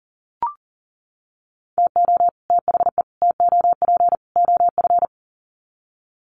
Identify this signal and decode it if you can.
Morse code, heard as a keyed tone, about 40 wpm, 700 Hz. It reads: TO THE TOP OF
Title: Sets of 4 Words 40wpm